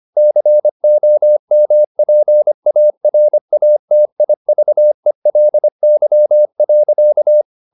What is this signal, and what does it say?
Morse code, heard as a keyed tone, about 25 wpm, 600 Hz. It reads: COMPARATIVELY.